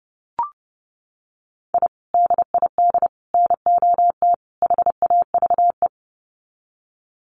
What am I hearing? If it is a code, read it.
Morse code, heard as a keyed tone, about 30 wpm, 700 Hz. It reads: I DID NOT HAVE
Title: Sets of 4 Words 30wpm